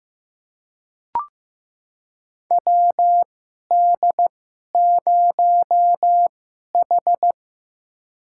Morse code, heard as a keyed tone, about 15 wpm, spaced as for 11 wpm, 700 Hz. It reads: WD0H
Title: USA Call Signs - 4 Characters - 2x Character Spacing - 15wpm